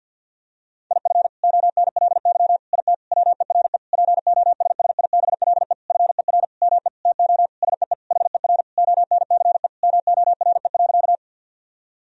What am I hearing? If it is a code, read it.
Morse code, heard as a keyed tone, about 51 wpm, 700 Hz. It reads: IF ONLY IT WERE POSSIBLE FER ME TO SEE HER ONCE MORE.
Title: Sentences from Top 500 Words and Abbreviations 50wpm